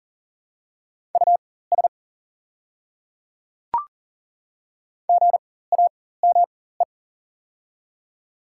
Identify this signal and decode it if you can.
Morse code, heard as a keyed tone, about 40 wpm, spaced as for 19 wpm, 700 Hz. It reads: US GAME
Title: 4x Character Spacing - Top 500 Words - 40wpm